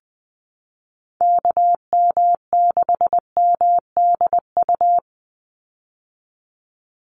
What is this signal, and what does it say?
Morse code, heard as a keyed tone, about 20 wpm, 700 Hz. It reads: KM6MDU
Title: USA Call Signs 20wpm